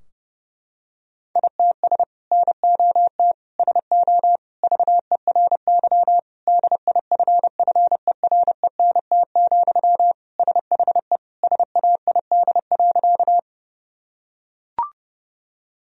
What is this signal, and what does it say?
Morse code, heard as a keyed tone, about 30 wpm, 700 Hz. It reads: ITS NOT SO VERY DIFFERENT, SHE SAID.
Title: Sentences from Top 1000 Words - Review Part 3 - 30wpm